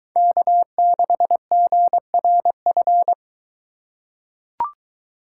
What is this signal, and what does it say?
Morse code, heard as a keyed tone, about 21 wpm, 700 Hz. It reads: K6GRF